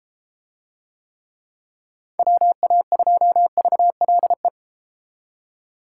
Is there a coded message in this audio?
Yes: Morse code from 2.2 to 4.5 s, heard as a keyed tone, about 32 wpm, 700 Hz. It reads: WA2VLE